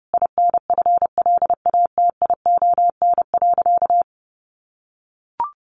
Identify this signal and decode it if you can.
Morse code, heard as a keyed tone, about 30 wpm, 700 Hz. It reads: INFLATION.
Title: Sentences from Top 1100 Words 30wpm